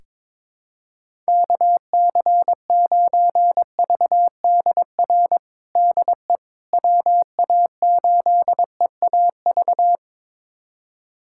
Morse code, heard as a keyed tone, about 22 wpm, 700 Hz. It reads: KC9VDR DE WA8EAV
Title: USA Callsign DE Callsign 22wpm